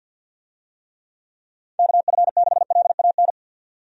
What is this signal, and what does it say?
Morse code, heard as a keyed tone, about 50 wpm, 700 Hz. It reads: KF6LAN
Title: USA Call Signs - 6 Characters - 50wpm